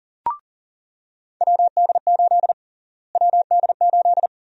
Morse code, heard as a keyed tone, about 40 wpm, 700 Hz. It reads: WD8 WD8
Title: USA Call Sign Prefixes - Repeated 2x - 40wpm